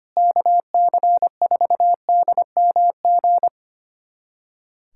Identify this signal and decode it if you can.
Morse code, heard as a keyed tone, about 25 wpm, 700 Hz. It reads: KC4DMG
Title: USA Callsign DE Callsign - 6 Characters - 25wpm